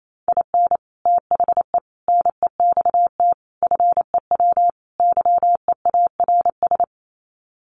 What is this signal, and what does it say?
Morse code, heard as a keyed tone, about 28 wpm, 700 Hz. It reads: IN THE NEXT FEW YEARS
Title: Sets of 5 Words Encore 28wpm